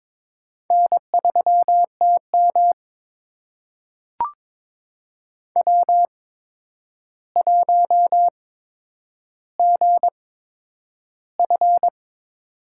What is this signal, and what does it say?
Morse code, heard as a keyed tone, about 23 wpm, spaced as for 9 wpm, 700 Hz. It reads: N3TM W1GF
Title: USA Call Signs - 4 Characters - 8x Character Spacing - 22wpm